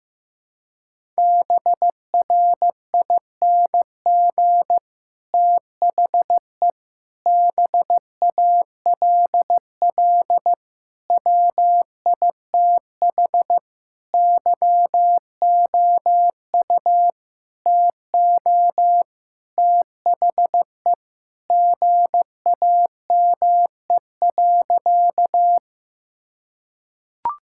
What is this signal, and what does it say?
Morse code, heard as a keyed tone, about 15 wpm, 700 Hz. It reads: BRING THE BALL WITH YOU TO THE GAME.
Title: Sentences from Top 1000 Words - Review Part 6 - 15wpm